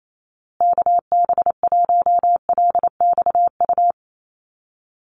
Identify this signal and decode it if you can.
Morse code, heard as a keyed tone, about 28 wpm, 700 Hz. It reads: KB1LXU
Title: USA Call Signs Repeated 3x 28wpm